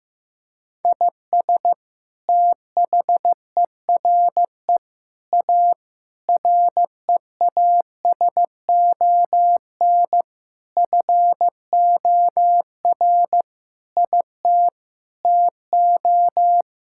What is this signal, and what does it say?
Morse code, heard as a keyed tone, about 14 wpm, 700 Hz. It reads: IS THERE A REASON FOR IT TO
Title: Sentences from Top 1000 Words - Review Part 5 - 15wpm